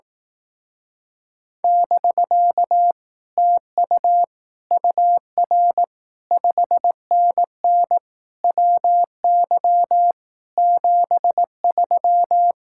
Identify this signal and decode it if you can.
Morse code, heard as a keyed tone, about 18 wpm, 700 Hz. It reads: <BK> TU UR 5NN WY 73
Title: Speed-Racing Common QSO Phrases Encore - 15wpm